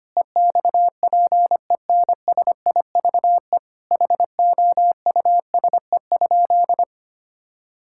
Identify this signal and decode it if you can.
Morse code, heard as a keyed tone, about 25 wpm, 700 Hz. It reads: EXPENSIVE HOUSE?
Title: Sentences from Top 1000 Words - Review Part 6 - 25wpm